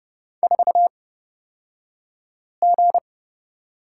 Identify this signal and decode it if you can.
Morse code, heard as a keyed tone, about 30 wpm, spaced as for 5 wpm, 700 Hz. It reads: 4G
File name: Lesson 38d - Letter q - Two Characters 30wpm